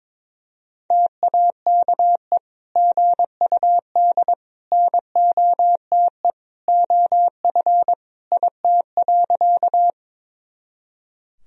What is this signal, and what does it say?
Morse code, heard as a keyed tone, about 22 wpm, 700 Hz. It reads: TAKE GUD NOTE OF IT.
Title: Sentences from Top 500 Words and Abbreviations 22wpm